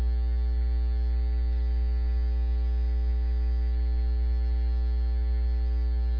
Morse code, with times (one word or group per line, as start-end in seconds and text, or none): none